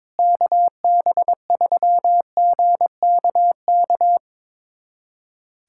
Morse code, heard as a keyed tone, about 22 wpm, 700 Hz. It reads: KB3GKK
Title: USA Callsign DE Callsign 22wpm